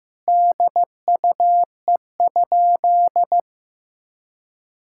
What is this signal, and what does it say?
Morse code, heard as a keyed tone, about 15 wpm, 700 Hz. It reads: DUE?
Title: Sentences from Top 1000 Words - Review Part 6 - 15wpm